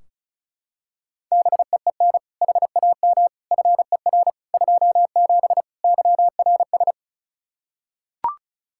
Morse code, heard as a keyed tone, about 34 wpm, 700 Hz. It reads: BEEN HAM FER 27 YRS
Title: Practice QSO - Exchange 3 - More Info - 35wpm